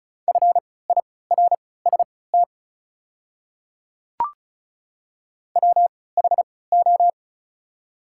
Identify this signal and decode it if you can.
Morse code, heard as a keyed tone, about 36 wpm, spaced as for 20 wpm, 690 Hz. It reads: FIRST WHO
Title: 3x Character Spacing - Top 100 Words - 35wpm